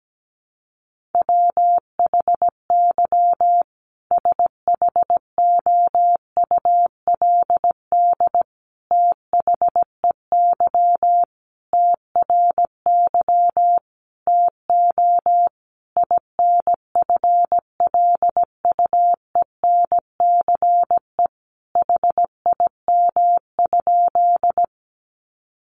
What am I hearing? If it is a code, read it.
Morse code, heard as a keyed tone, about 17 wpm, 700 Hz. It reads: WHY SHOULD THEY TRY TO INFLUENCE HIM?
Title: Sentences from Top 1100 Words 17wpm